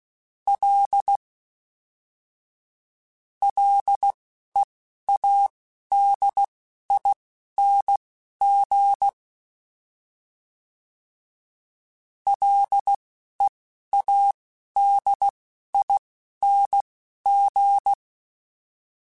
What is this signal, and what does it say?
Morse code, heard as a keyed tone, about 16 wpm, spaced as for 12 wpm, 800 Hz. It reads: L LEADING LEADING
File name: Ls / Single L m3s / Single L Prefixe 3x2 Twice @10